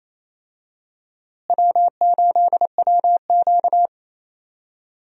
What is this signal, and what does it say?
Morse code, heard as a keyed tone, about 28 wpm, 700 Hz. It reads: W8WQ
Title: USA Call Signs - 4 Characters - Repeated 2x - 28wpm